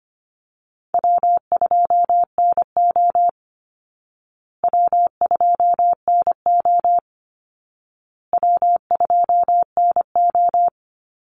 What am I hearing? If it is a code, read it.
Morse code, heard as a keyed tone, about 25 wpm, 700 Hz. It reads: W2NO W2NO W2NO
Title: Easy Call Signs Repeated 3x - 25wpm